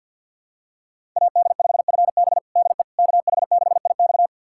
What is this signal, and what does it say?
Morse code, heard as a keyed tone, about 47 wpm, 700 Hz. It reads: AD5FB DE KH6IX
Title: USA Callsign DE Callsign - 5 Characters - 50wpm